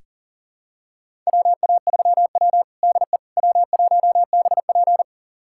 Morse code, heard as a keyed tone, about 37 wpm, 700 Hz. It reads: WA3W DE W1BP
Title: USA Callsign DE Callsign - 4 Characters - 40wpm